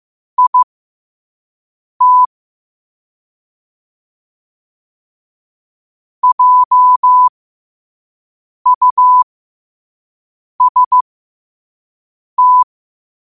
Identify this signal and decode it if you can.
Morse code, heard as a keyed tone, about 15 wpm, spaced as for 5 wpm, 1000 Hz. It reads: IT JUST
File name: qotdfiles / 5wpm